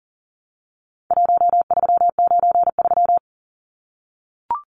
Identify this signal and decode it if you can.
Morse code, heard as a keyed tone, about 40 wpm, 700 Hz. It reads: 1393
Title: Numbers - 4-Digit 40wpm